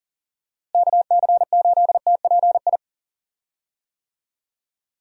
Morse code, heard as a keyed tone, about 40 wpm, 700 Hz. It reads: KC8TPI
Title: USA Call Signs - 6 Characters - Repeated 2x - 40wpm